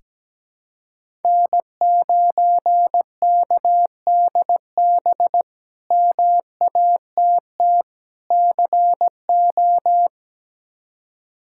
Morse code, heard as a keyed tone, about 17 wpm, 700 Hz. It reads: N9KDB MATT CO
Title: K1USN SST QSO Practice - 17wpm